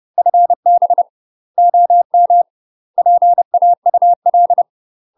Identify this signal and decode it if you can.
Morse code, heard as a keyed tone, about 30 wpm, 700 Hz. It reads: FB OM PAUL
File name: UK MASTER QSOs TRAINING/30WPM